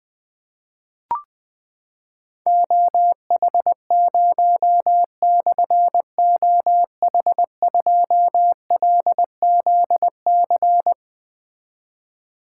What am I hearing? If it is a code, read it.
Morse code, heard as a keyed tone, about 20 wpm, 700 Hz. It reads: OH0/OH2LZC
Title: Difficult Call Signs 20wpm